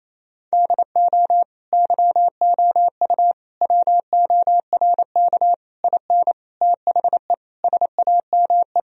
Morse code, heard as a keyed tone, about 28 wpm, 700 Hz. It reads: DO YOU WORK IN THE SAME
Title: Sentences from Top 1000 Words - Review Part 4 - 28wpm